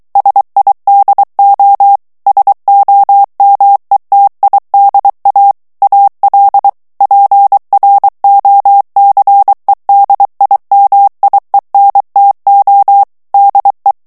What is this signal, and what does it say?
Morse code, heard as a keyed tone, about 23 wpm, 800 Hz. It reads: SIDO SOMETIDA AL PROCEDIMIENTO DE
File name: morse mp3 / ESPAGNOL TEXTE EN CLAIR 2 VITESSE 22 MOTS MINUTES